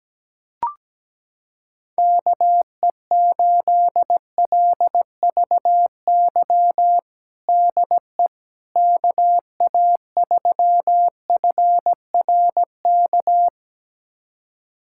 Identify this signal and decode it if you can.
Morse code, heard as a keyed tone, about 17 wpm, 700 Hz. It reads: KE8LVY DE KA3FRK
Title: USA Callsign DE Callsign - 6 Characters - 17wpm